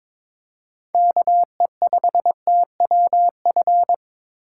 Morse code, heard as a keyed tone, about 22 wpm, 700 Hz. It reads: KE5TWF
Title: USA Call Signs - 6 Characters - 2x Character Spacing - 22wpm